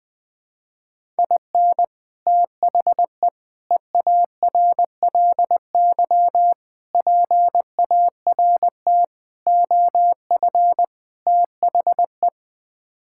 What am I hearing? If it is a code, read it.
Morse code, heard as a keyed tone, about 20 wpm, 700 Hz. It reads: IN THE EARLY PART OF THE
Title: Sets of 6 Words - 20wpm